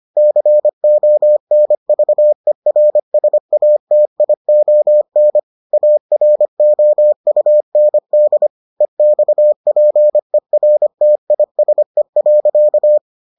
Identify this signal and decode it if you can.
Morse code, heard as a keyed tone, about 25 wpm, 600 Hz. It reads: CONVERSATION AROUND EXPERTISE.